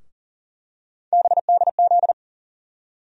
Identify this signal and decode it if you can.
Morse code, heard as a keyed tone, about 40 wpm, 700 Hz. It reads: BDZ
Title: USA Call Sign Postfixes - 3 Letters 40wpm